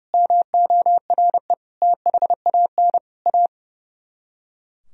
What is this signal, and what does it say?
Morse code, heard as a keyed tone, about 40 wpm, 700 Hz. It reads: MORE THAN A